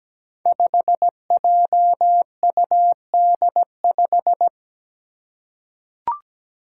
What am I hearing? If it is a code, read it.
Morse code, heard as a keyed tone, about 17 wpm, 700 Hz. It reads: 5JUD5
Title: Five Challenging Letter-Number Combinations 17wpm